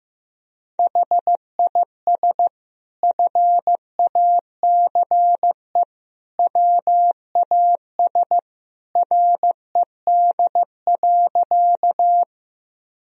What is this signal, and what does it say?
Morse code, heard as a keyed tone, about 15 wpm, 700 Hz. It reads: HIS FACE WAS RED.
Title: Sentences from Top 1000 Words - Review Part 2 - 15wpm